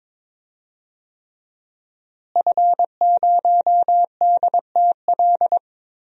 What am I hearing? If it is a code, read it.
Morse code, heard as a keyed tone, about 22 wpm, 700 Hz. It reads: F0DTL
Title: DXCC-100 Call Signs - 5 Characters - Repeated 3x - 22wpm